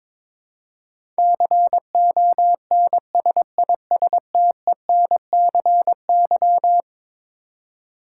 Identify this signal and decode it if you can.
Morse code, heard as a keyed tone, about 22 wpm, 700 Hz. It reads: CONSISTENCY
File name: practice / 22wpm / Eleven Letter Words - 22wpm